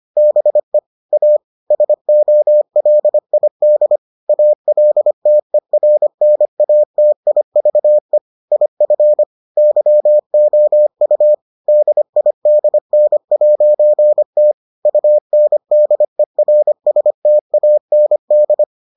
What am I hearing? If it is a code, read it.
Morse code, heard as a keyed tone, about 25 wpm, 600 Hz. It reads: BE A SOLID ALTERNATIVE IF YOU DIDN'T UNDERSTAND